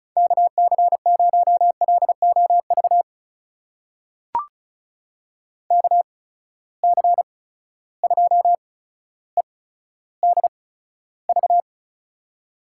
Morse code, heard as a keyed tone, about 36 wpm, spaced as for 15 wpm, 700 Hz. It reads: KC0LOV KC2EDV
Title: USA Call Signs - 6 Characters - 8x Character Spacing - 35wpm